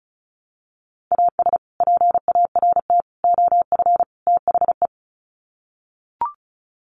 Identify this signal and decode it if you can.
Morse code, heard as a keyed tone, about 34 wpm, 700 Hz. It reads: AS PART OF THE
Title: Sets of 4 Words 35wpm